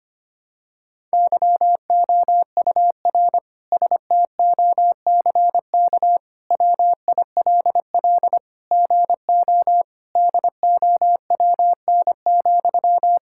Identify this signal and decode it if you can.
Morse code, heard as a keyed tone, about 25 wpm, 700 Hz. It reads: YOUR STOCK WILL GO DOWN,
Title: Sentences from Top 1000 Words - Review Part 5 - 25wpm